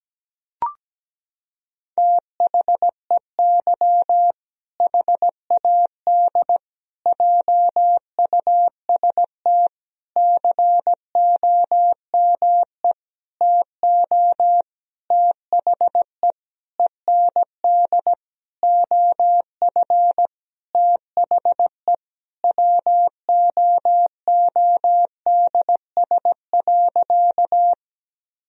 Morse code, heard as a keyed tone, about 17 wpm, 700 Hz. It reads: THEY HAD JUST COME TO THE END OF THE WOODS.
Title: Sentences from Top 1000 Words - Review Part 2 - 17wpm